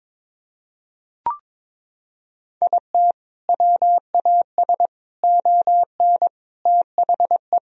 Morse code, heard as a keyed tone, about 21 wpm, 700 Hz. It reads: IT WAS ON THE
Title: Sentences from Top 1000 Words - Review Part 2 - 22wpm